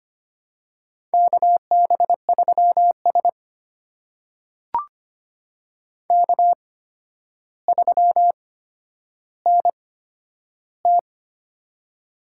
Morse code, heard as a keyed tone, about 25 wpm, spaced as for 11 wpm, 700 Hz. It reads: KB3S K3NT
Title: USA Call Signs - 4 Characters - 8x Character Spacing - 25wpm